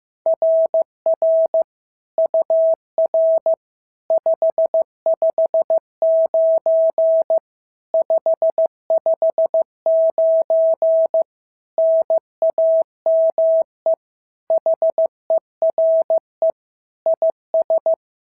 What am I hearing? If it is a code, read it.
Morse code, heard as a keyed tone, about 15 wpm, 650 Hz. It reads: RR UR 559 559 NAME HERE IS
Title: SOTA QSO Practice 15wpm